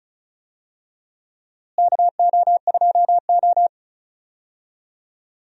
Morse code, heard as a keyed tone, about 36 wpm, 700 Hz. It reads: KO2O
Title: USA Call Signs - 4 Characters - Repeated 2x - 35wpm